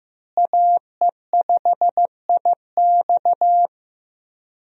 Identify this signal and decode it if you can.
Morse code, heard as a keyed tone, about 15 wpm, 700 Hz. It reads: AE5IX
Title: USA Call Signs - 5 Characters - Repeated 3x - 15wpm